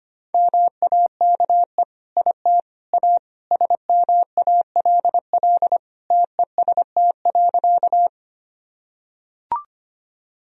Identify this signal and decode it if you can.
Morse code, heard as a keyed tone, about 25 wpm, 700 Hz. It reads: MAKE IT A SMALL TEST.
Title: Sentences from Top 1000 Words - Review Part 3 - 25wpm